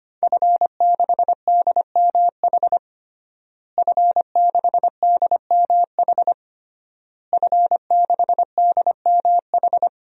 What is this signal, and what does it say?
Morse code, heard as a keyed tone, about 25 wpm, 700 Hz. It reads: F6DMH F6DMH F6DMH
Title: DXCC-100 Call Signs - 5 Characters - Repeated 3x - 25wpm